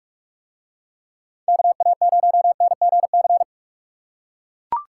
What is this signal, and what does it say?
Morse code, heard as a keyed tone, about 44 wpm, 700 Hz. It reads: KA0NGC